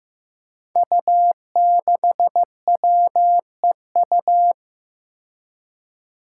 Morse code, heard as a keyed tone, about 15 wpm, 700 Hz. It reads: U6WEU
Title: Five Challenging Letter-Number Combinations 15wpm